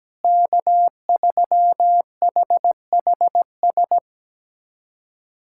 Morse code, heard as a keyed tone, about 17 wpm, 700 Hz. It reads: K3HHS
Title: USA Callsign DE Callsign - 5 Characters - 17wpm